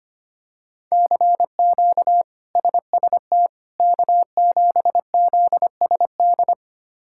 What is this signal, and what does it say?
Morse code, heard as a keyed tone, about 25 wpm, 700 Hz. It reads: CQ SST K7ZSD
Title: K1USN SST QSO Practice - 25wpm